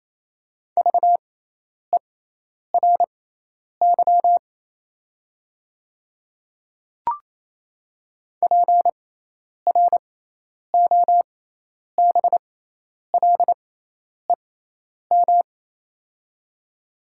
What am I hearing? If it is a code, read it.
Morse code, heard as a keyed tone, about 28 wpm, spaced as for 10 wpm, 700 Hz. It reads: VERY PROBLEM
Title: 6x Character Spacing - Top 500 Words - 28wpm